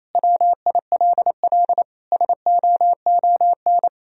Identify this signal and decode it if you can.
Morse code, heard as a keyed tone, about 28 wpm, 700 Hz. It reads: WILL SOON
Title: Sentences from Top 400 Words and Abbreviations 28wpm